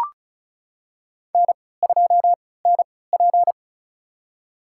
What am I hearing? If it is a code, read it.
Morse code, heard as a keyed tone, about 36 wpm, spaced as for 20 wpm, 700 Hz. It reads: N2NP
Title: USA Call Signs - 4 Characters - 3x Character Spacing - 35wpm